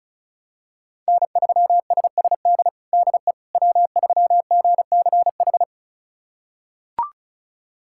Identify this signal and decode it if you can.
Morse code, heard as a keyed tone, about 36 wpm, 700 Hz. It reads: N3SSD DE W3GCH
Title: USA Callsign DE Callsign - 5 Characters - 35wpm